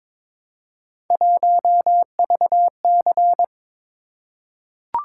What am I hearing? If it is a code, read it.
Morse code, heard as a keyed tone, about 22 wpm, 700 Hz. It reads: 1VC
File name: practice / 22wpm / Three Letter-Number Combinations - 22wpm